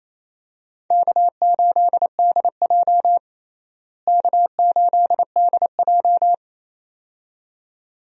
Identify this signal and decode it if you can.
Morse code, heard as a keyed tone, about 27 wpm, 700 Hz. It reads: K8DJ K8DJ
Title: USA Call Signs - 4 Characters - Repeated 2x - 28wpm